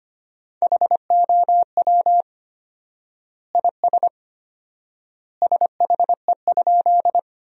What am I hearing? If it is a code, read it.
Morse code, heard as a keyed tone, about 25 wpm, 700 Hz. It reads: HOW IS SHE?
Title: 4x Word-Spacing - Sentences from Top 100 Words - 25wpm